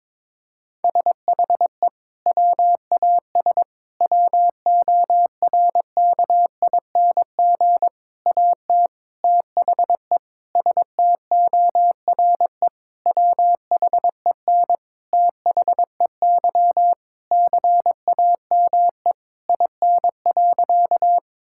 Morse code, heard as a keyed tone, about 22 wpm, 700 Hz. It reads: SHE WAS WORKING AT THE STORE WHEN THEY CAME IN.
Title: Sentences from Top 1000 Words - Review Part 5 - 22wpm